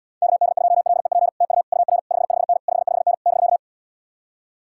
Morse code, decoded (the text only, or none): KD2LEP TU UR 55N 55N <BK>